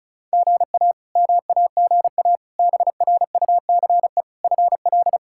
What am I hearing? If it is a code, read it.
Morse code, heard as a keyed tone, about 36 wpm, 700 Hz. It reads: GA MAGA BRUCE FL
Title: K1USN SST QSO Practice - 35wpm